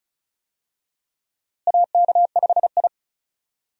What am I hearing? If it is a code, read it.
Morse code, heard as a keyed tone, about 36 wpm, 700 Hz. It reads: AK5I
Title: USA Call Signs - 4 Characters - Repeated 3x - 35wpm